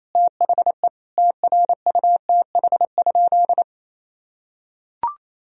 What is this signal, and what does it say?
Morse code, heard as a keyed tone, about 28 wpm, 700 Hz. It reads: THE TRUTH?
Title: Sentences from Top 1000 Words - Review Part 3 - 28wpm